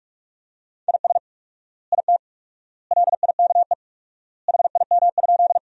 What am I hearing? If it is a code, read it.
Morse code, heard as a keyed tone, about 47 wpm, 700 Hz. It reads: IS IT LIKE HIM?
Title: 4x Word-Spacing - Sentences from Top 100 Words - 45wpm